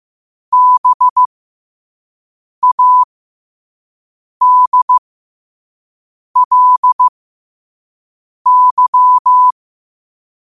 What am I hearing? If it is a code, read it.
Morse code, heard as a keyed tone, about 15 wpm, spaced as for 5 wpm, 1000 Hz. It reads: BADLY
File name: qotdfiles / 5wpm